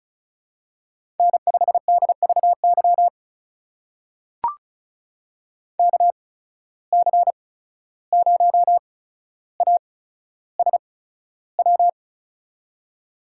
Morse code, heard as a keyed tone, about 36 wpm, spaced as for 14 wpm, 700 Hz. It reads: N5DVY KC0ASW
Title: USA Call Signs - 8x Character Spacing - 35wpm